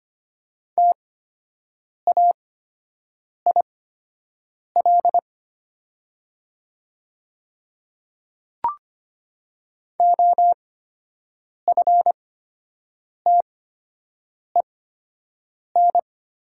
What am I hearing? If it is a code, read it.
Morse code, heard as a keyed tone, about 25 wpm, spaced as for 7 wpm, 700 Hz. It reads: TAIL OFTEN
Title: 8x Character Spacing - Top 500 Words - 25wpm